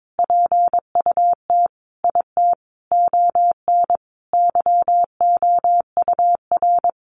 Morse code, heard as a keyed tone, about 22 wpm, 700 Hz. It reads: PUT IT ON YOUR
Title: Sentences from Top 1400 Words 22wpm